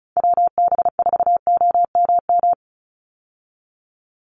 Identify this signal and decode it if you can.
Morse code, heard as a keyed tone, about 34 wpm, 700 Hz. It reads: WB4OMM
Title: Call Signs Repeated 3x - 35wpm